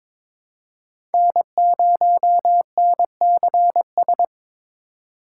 Morse code, heard as a keyed tone, about 22 wpm, 700 Hz. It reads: N0NCS
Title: USA Call Signs - 5 Characters - 4x Character Spacing - 22wpm